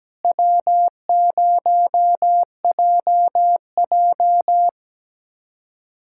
Morse code, heard as a keyed tone, about 17 wpm, 700 Hz. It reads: W0JJ
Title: USA Callsign DE Callsign - 4 Characters - 17wpm